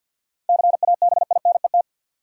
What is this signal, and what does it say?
Morse code, heard as a keyed tone, about 50 wpm, 700 Hz. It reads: CABINET
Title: Seven Letter Words - 50wpm